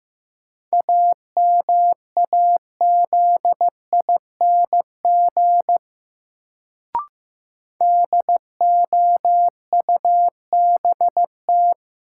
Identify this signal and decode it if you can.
Morse code, heard as a keyed tone, about 15 wpm, 700 Hz. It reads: AMAZING DOUBT
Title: Sentences from Top 1500 Words - New Word Drill Alt - 15wpm